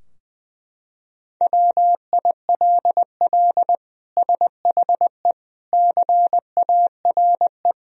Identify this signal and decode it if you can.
Morse code, heard as a keyed tone, about 20 wpm, 700 Hz. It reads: WILL SHE CARE
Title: Sentences from Top 1000 Words - Review Part 3 - 20wpm